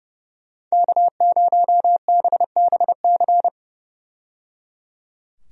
Morse code, heard as a keyed tone, about 30 wpm, 700 Hz. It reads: K0BBC
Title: USA Call Signs - 5 Characters - 30wpm